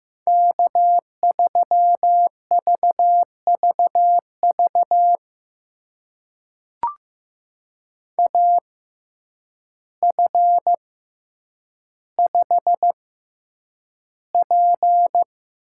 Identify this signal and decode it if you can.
Morse code, heard as a keyed tone, about 15 wpm, spaced as for 8 wpm, 700 Hz. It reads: K3VVV AF5P
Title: USA Call Signs - 5 Characters - 6x Character Spacing - 15wpm